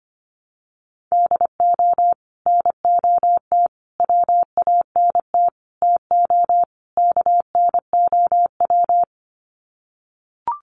Morse code, heard as a keyed tone, about 25 wpm, 700 Hz. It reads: DO NOT WANT TO KNOW
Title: Sets of 5 Words Encore 25wpm